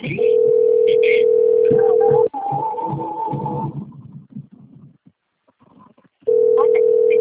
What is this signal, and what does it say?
Ringback tone (North America), 440 + 480 Hz, with speech.
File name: rg-1600-unknown- - - .8094